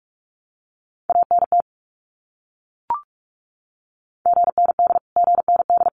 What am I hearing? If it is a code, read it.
Morse code, heard as a keyed tone, about 44 wpm, 700 Hz. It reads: ANT GND GND
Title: Abbreviations Repeated 3x - 45wpm